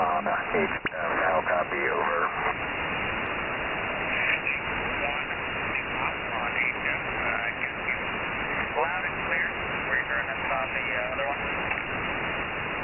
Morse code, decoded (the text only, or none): none